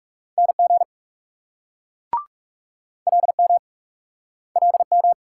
Morse code, heard as a keyed tone, about 35 wpm, 700 Hz. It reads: NG LM LM
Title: Speed-Racing Multi-letter Phonemes - 30wpm